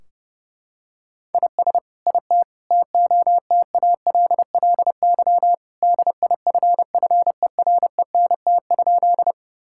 Morse code, heard as a keyed tone, about 30 wpm, 700 Hz. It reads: IS IT TOTALLY DIFFERENT?
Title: Sentences from Top 1600 Words 30wpm